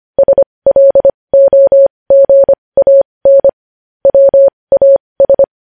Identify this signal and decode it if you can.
Morse code, heard as a keyed tone, about 25 wpm, spaced as for 20 wpm, 550 Hz. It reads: SLOGAN WAS
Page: news-headlines-in-morse-code-at-20-wpm